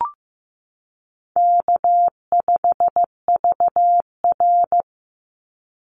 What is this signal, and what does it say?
Morse code, heard as a keyed tone, about 15 wpm, 700 Hz. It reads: K5VR
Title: Easy Call Signs - 15wpm